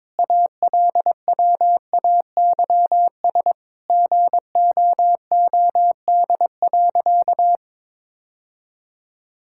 Morse code, heard as a keyed tone, about 22 wpm, 700 Hz. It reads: ALWAYS GOOD.
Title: Sentences from Top 1000 Words - Review Part 5 - 22wpm